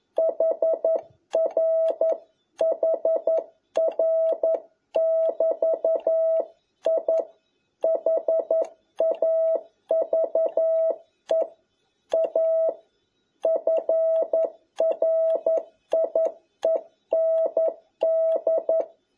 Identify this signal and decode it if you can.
Morse code, heard as a keyed tone, about 11 wpm, 650 Hz. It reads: HR HR <BT> I HAVE A FRIEND
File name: jh8cbh / 2021cw